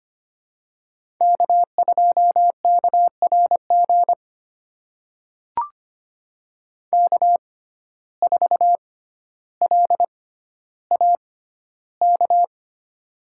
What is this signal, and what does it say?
Morse code, heard as a keyed tone, about 25 wpm, spaced as for 13 wpm, 700 Hz. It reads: K2KRG K4LAK
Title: USA Call Signs - 5 Characters - 6x Character Spacing - 25wpm